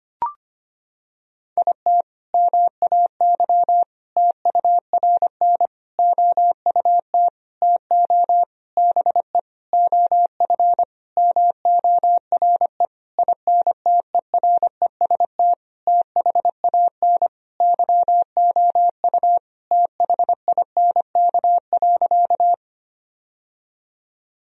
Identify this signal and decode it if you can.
Morse code, heard as a keyed tone, about 25 wpm, 700 Hz. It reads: IT MAY TURN OUT TO BE OF MORE INTEREST THAN YOU THINK.
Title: Sentences from Top 1000 Words - Review Part 2 - 25wpm